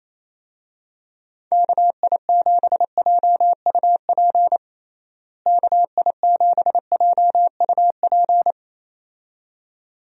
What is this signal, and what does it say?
Morse code, heard as a keyed tone, about 28 wpm, 700 Hz. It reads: KI7JUP KI7JUP
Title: USA Call Signs - 6 Characters - Repeated 2x - 28wpm